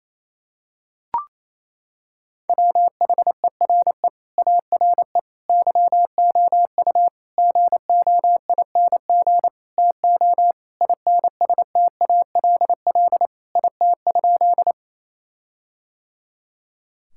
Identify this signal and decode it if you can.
Morse code, heard as a keyed tone, about 28 wpm, 700 Hz. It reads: WHERE ARE YOU GOING TO INSTALL IT?
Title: Sentences from Top 1600 Words 28wpm